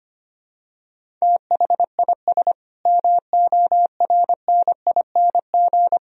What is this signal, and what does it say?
Morse code, heard as a keyed tone, about 25 wpm, 700 Hz. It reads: THIS MORNING
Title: Sentences from Top 1000 Words - Review Part 3 - 25wpm